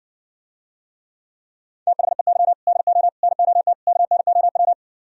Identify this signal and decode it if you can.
Morse code, heard as a keyed tone, about 68 wpm, 700 Hz. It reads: THEY DO NOT KNOW